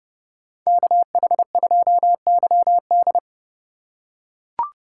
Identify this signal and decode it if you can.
Morse code, heard as a keyed tone, about 30 wpm, 700 Hz. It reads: KH2YD